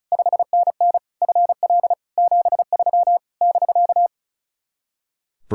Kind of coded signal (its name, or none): Morse code